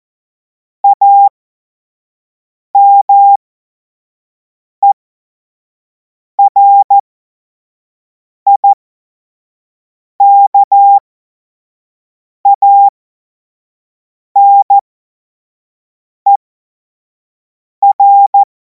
Morse code, heard as a keyed tone, about 14 wpm, spaced as for 5 wpm, 800 Hz. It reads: AMERIKANER
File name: Morsekurs / Witze / Witz 16. Juni 2026 25 F70